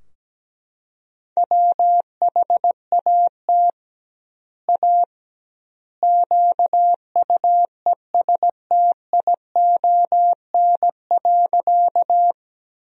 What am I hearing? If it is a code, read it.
Morse code, heard as a keyed tone, about 17 wpm, 700 Hz. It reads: WHAT A QUESTION.
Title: 2x Word-Spacing - Sentences from Top 400 Words - 17wpm